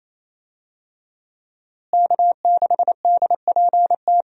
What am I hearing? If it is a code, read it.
Morse code, heard as a keyed tone, about 27 wpm, 700 Hz. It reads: K6DPT
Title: USA Call Signs - 5 Characters - Repeated 2x - 28wpm